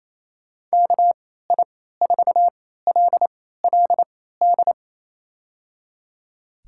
Morse code, heard as a keyed tone, about 27 wpm, spaced as for 16 wpm, 700 Hz. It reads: KI4LLD
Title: USA Call Signs - 3x Character Spacing - 28wpm